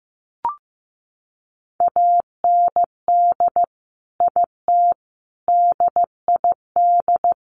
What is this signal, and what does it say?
Morse code, heard as a keyed tone, about 15 wpm, 700 Hz. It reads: AND IT DID
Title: Sentences from Top 200 Words 15wpm